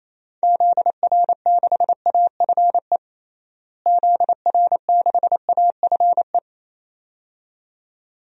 Morse code, heard as a keyed tone, about 28 wpm, 700 Hz. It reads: ZR6AFE ZR6AFE
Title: DXCC-100 Call Signs - 6 Characters - Repeated 2x - 28wpm